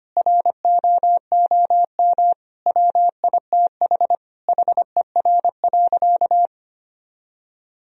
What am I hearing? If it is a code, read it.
Morse code, heard as a keyed tone, about 25 wpm, 700 Hz. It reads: ROOM WITH HER.
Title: Sentences from Top 1000 Words - Review Part 2 - 25wpm